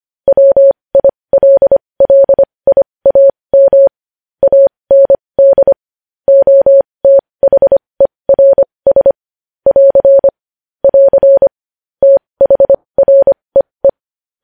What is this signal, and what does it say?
Morse code, heard as a keyed tone, about 25 wpm, spaced as for 20 wpm, 550 Hz. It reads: WILLIAM AND OTHERS <AR> <AR> THREE
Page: news headlines in morse code at 20 wpm